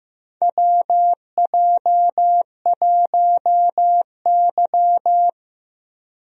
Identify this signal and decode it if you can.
Morse code, heard as a keyed tone, about 15 wpm, 700 Hz. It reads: WJ1Y
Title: USA Call Signs - 4 Characters - Repeated 3x - 15wpm